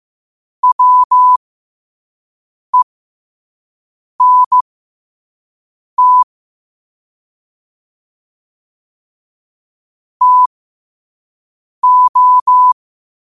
Morse code, heard as a keyed tone, about 15 wpm, spaced as for 5 wpm, 1000 Hz. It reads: WENT TO
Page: New quotes every day in morse code at 5 Words per minute.